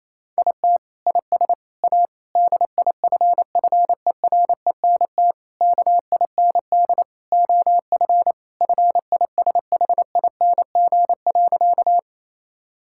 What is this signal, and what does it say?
Morse code, heard as a keyed tone, about 27 wpm, 700 Hz. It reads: IT IS A DIFFERENT KIND OF FISHING.
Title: Sentences from Top 1000 Words - Review Part 5 - 28wpm